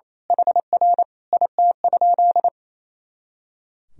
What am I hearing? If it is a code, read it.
Morse code, heard as a keyed tone, about 28 wpm, 700 Hz. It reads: HR IT?
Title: Sentences from Top 300 Words and Abbreviations 28wpm